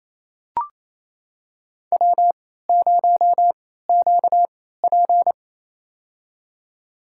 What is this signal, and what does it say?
Morse code, heard as a keyed tone, about 28 wpm, spaced as for 16 wpm, 700 Hz. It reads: W0QP
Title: USA Call Signs - 4 Characters - 3x Character Spacing - 28wpm